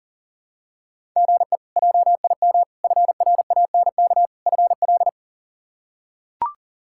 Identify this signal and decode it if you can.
Morse code, heard as a keyed tone, about 40 wpm, 700 Hz. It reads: GE JIM FRANK FL
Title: K1USN SST QSO Practice - 40wpm